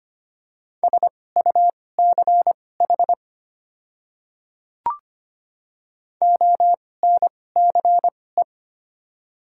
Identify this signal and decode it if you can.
Morse code, heard as a keyed tone, about 25 wpm, spaced as for 18 wpm, 700 Hz. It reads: SUCH ONCE
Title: 2x Character Spacing - Top 300 Words - 25wpm